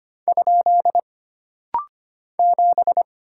Morse code, heard as a keyed tone, about 25 wpm, 700 Hz. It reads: ? 7
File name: Single Character (Letter:Number:Punctuation:Prosign) - Rapid-Fire - 25wpm